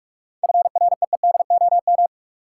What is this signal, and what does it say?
Morse code, heard as a keyed tone, about 46 wpm, 700 Hz. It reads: FREEDOM
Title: Sentences from Top 800 Words - New Word Drill - 45wpm